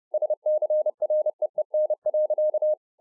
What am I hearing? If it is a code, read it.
Morse code, heard as a keyed tone, about 30 wpm, 610 Hz. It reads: SCREEN.